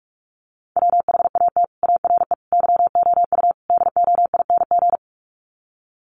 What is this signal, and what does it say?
Morse code, heard as a keyed tone, about 44 wpm, 700 Hz. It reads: WHAT ARE YOU DOING
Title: Sets of 4 Words 45wpm